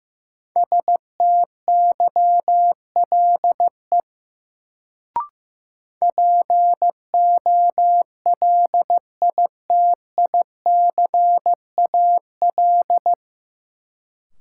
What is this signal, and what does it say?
Morse code, heard as a keyed tone, about 15 wpm, 700 Hz. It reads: STYLE POLITICAL
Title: Sentences from Top 800 Words - New Word Drill - 15wpm